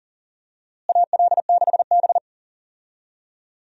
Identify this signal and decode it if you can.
Morse code, heard as a keyed tone, about 40 wpm, 700 Hz. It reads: AL6B
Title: USA Call Signs - 4 Characters - Repeated 3x - 40wpm